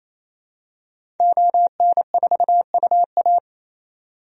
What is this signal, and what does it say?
Morse code, heard as a keyed tone, about 28 wpm, 700 Hz. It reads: ON4UA
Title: DXCC-100 Call Signs - 5 Characters - 28wpm